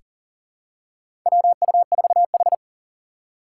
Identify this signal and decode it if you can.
Morse code, heard as a keyed tone, about 37 wpm, 700 Hz. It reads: WU4H
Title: USA Call Signs - 4 Characters - 6x Character Spacing - 40wpm